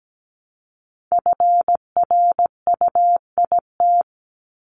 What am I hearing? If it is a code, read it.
Morse code, heard as a keyed tone, about 17 wpm, 700 Hz. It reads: FRUIT
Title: Top 1251-1500 Words 17wpm